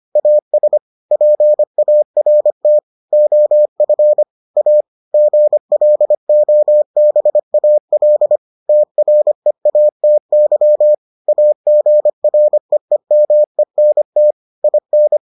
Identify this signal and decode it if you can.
Morse code, heard as a keyed tone, about 25 wpm, 600 Hz. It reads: AS PART OF A GLOBAL TREATY AGREEMENT IN